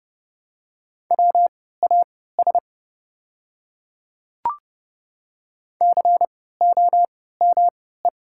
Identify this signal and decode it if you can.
Morse code, heard as a keyed tone, about 30 wpm, spaced as for 17 wpm, 700 Hz. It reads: WAS COME
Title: 3x Character Spacing - Top 100 Words - 30wpm